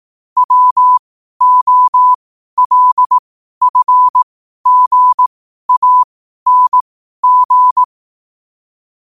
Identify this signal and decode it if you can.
Morse code, heard as a keyed tone, about 18 wpm, spaced as for 12 wpm, 1000 Hz. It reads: WOLFGANG
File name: qotdfiles / 12wpm